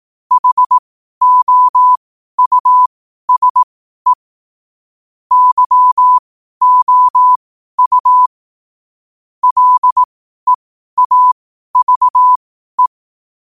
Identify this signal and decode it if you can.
Morse code, heard as a keyed tone, about 18 wpm, spaced as for 12 wpm, 1000 Hz. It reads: HOUSE YOU LEAVE
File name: qotdfiles / 12wpm